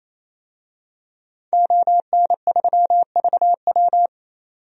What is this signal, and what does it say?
Morse code, heard as a keyed tone, about 28 wpm, 700 Hz. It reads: ON3VW
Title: DXCC-100 Call Signs - 5 Characters - 28wpm